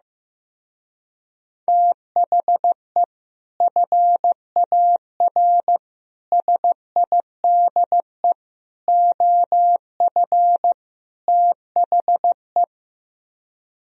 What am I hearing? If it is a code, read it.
Morse code, heard as a keyed tone, about 15 wpm, 700 Hz. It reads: THE FAR SIDE OF THE
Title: Speed-Racing - Sets of 5 Words 15wpm